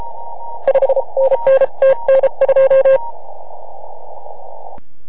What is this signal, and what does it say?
Morse code, heard as a keyed tone, about 33 wpm, 540 Hz. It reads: 5NN TN2